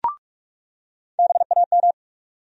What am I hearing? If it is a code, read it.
Morse code, heard as a keyed tone, about 43 wpm, 700 Hz. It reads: BAM
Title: USA Call Sign Postfixes - 3 Letters 45wpm